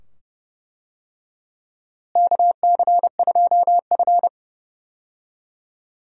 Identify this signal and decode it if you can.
Morse code, heard as a keyed tone, about 31 wpm, 700 Hz. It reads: KC2F
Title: USA Call Signs - 4 Characters - Repeated 2x - 30wpm